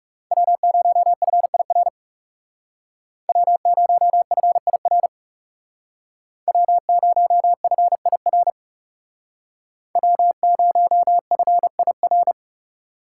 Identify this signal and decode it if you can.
Morse code, heard as a keyed tone, about 38 wpm, 700 Hz. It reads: W0FIR W0FIR W0FIR W0FIR